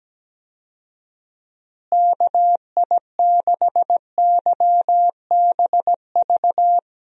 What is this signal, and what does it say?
Morse code, heard as a keyed tone, about 17 wpm, 700 Hz. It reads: KI6YBV
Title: USA Call Signs - 6 Characters - Repeated 3x - 17wpm